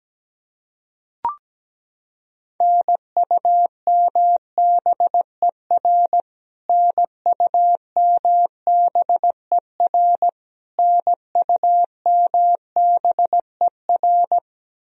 Morse code, heard as a keyed tone, about 17 wpm, 700 Hz. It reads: NUMBER NUMBER NUMBER
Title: Top 100 Words Repeated 17wpm